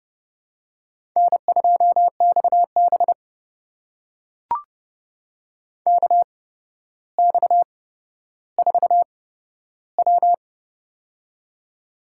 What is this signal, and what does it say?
Morse code, heard as a keyed tone, about 30 wpm, spaced as for 13 wpm, 700 Hz. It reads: N2XB KX4W
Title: USA Call Signs - 4 Characters - 8x Character Spacing - 30wpm